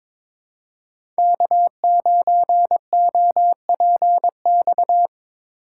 Morse code, heard as a keyed tone, about 25 wpm, 700 Hz. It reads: K9OPX
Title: Speed Racing - USA 1x3 Call Signs 22wpm